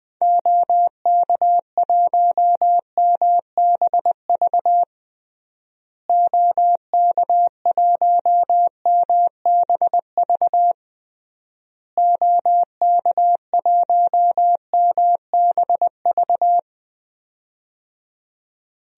Morse code, heard as a keyed tone, about 20 wpm, 700 Hz. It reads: OK1MBV OK1MBV OK1MBV
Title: DXCC-100 Call Signs - 6 Characters - Repeated 3x - 20wpm